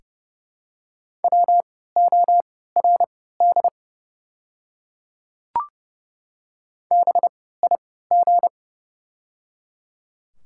Morse code, heard as a keyed tone, about 30 wpm, spaced as for 17 wpm, 700 Hz. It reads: WORD BIG
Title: 3x Character Spacing - Top 500 Words - 30wpm